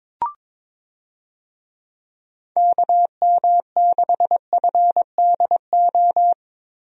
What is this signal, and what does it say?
Morse code, heard as a keyed tone, about 22 wpm, 700 Hz. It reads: KM6FDO
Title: USA Call Signs - 6 Characters - Repeated 2x - 22wpm